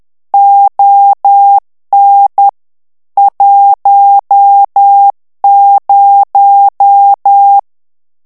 Morse code, heard as a keyed tone, about 10 wpm, 800 Hz. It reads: ON 10
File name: ANGLAIS texte clair 3 vitesse 10 mots minute